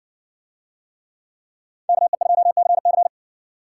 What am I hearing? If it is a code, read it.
Morse code, heard as a keyed tone, about 60 wpm, 700 Hz. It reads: KE2CC